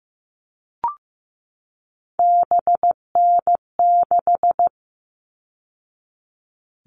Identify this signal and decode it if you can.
Morse code, heard as a keyed tone, about 15 wpm, 700 Hz. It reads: BN6
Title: All Cut Numbers - 3-Digit 15wpm